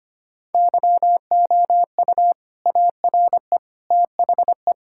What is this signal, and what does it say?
Morse code, heard as a keyed tone, about 25 wpm, 700 Hz. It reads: YOU ARE THE
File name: practice / 25wpm / Sentences with 2-Word Phrases - 25wpm